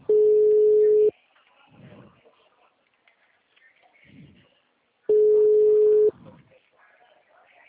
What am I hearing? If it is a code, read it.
Ringback tone (Europe, 425 Hz), with speech.